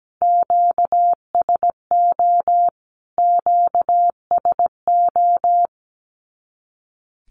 Morse code, heard as a keyed tone, about 17 wpm, 700 Hz. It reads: QSO QSO
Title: Q-Codes Repeated 3x 17wpm